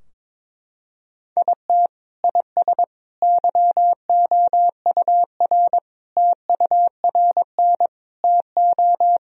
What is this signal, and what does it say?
Morse code, heard as a keyed tone, about 21 wpm, 700 Hz. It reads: IT IS YOUR TURN TO
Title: Sentences from Top 1000 Words - Review Part 2 - 22wpm